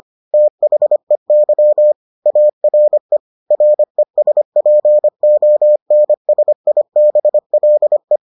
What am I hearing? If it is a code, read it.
Morse code, heard as a keyed tone, about 25 wpm, 600 Hz. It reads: THEY ARE RESPONSIBLE